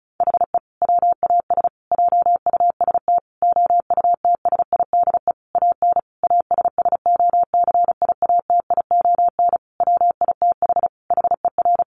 Morse code, heard as a keyed tone, about 34 wpm, 700 Hz. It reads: HE WAS JUST OUTSIDE AN ASSOCIATION WITH HER
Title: Sentences from Top 1100 Words 35wpm